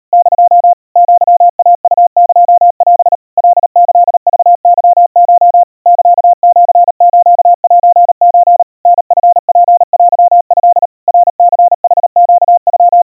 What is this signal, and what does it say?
Morse code (Japanese Wabun code), heard as a keyed tone, about 37 wpm, 700 Hz. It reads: <DO> アイウエオ カキクケコ サシスセソ タチツテト ナニヌネノ
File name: aiueo 130 cpm 700 Hz Crispy